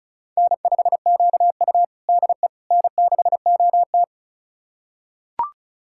Morse code, heard as a keyed tone, about 36 wpm, 700 Hz. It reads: N5QU DE N6OT
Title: USA Callsign DE Callsign - 4 Characters - 35wpm